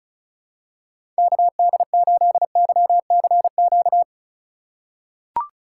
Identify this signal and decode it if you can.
Morse code, heard as a keyed tone, about 34 wpm, 700 Hz. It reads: KD8YCQ
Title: USA Call Signs - 3x Character Spacing - 35wpm